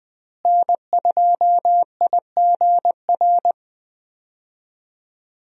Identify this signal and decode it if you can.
Morse code, heard as a keyed tone, about 20 wpm, 700 Hz. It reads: N2IGR
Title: USA Call Signs - 5 Characters - Repeated 2x - 20wpm